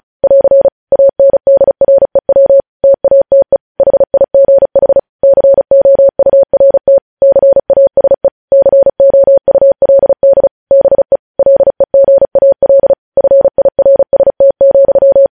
Morse code, heard as a keyed tone, about 36 wpm, 550 Hz. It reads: <AR> ANDREW TATE HIGH COURT CASE COULD BE LEGAL FIRST,
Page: news headlines in morse code at 35 wpm